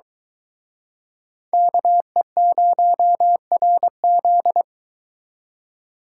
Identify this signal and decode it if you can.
Morse code, heard as a keyed tone, about 24 wpm, 700 Hz. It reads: KE0RZ